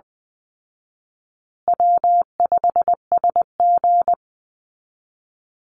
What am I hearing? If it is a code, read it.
Morse code, heard as a keyed tone, about 20 wpm, 700 Hz. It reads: W5SG
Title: Easy Call Signs Repeated 2x - 20wpm